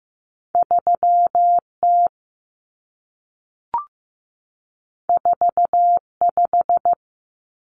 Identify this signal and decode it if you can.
Morse code, heard as a keyed tone, about 15 wpm, 700 Hz. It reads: 3T 45
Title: Common Cut Numbers - 2-Digit 15wpm